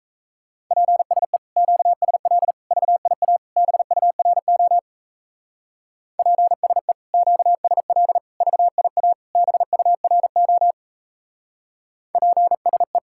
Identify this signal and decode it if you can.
Morse code, heard as a keyed tone, about 36 wpm, 700 Hz. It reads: PSE QSL VIA BURO PSE QSL VIA BURO PSE